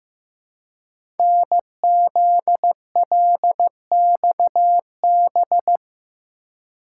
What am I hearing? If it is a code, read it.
Morse code, heard as a keyed tone, about 15 wpm, 700 Hz. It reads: NZLXB
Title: Code Groups - 5 Letters - 15wpm